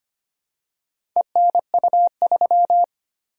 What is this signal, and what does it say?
Morse code, heard as a keyed tone, about 25 wpm, 700 Hz. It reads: ENU3
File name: practice / 25wpm / Four Letter-Number Combinations - 25wpm